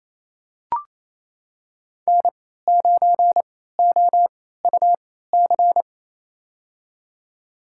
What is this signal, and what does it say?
Morse code, heard as a keyed tone, about 28 wpm, spaced as for 16 wpm, 700 Hz. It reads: N9OUC
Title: USA Call Signs - 3x Character Spacing - 28wpm